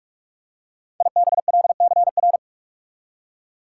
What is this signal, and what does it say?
Morse code, heard as a keyed tone, about 46 wpm, 700 Hz. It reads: IBLCR
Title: Code Groups - 5 Letter:Numbers - 45wpm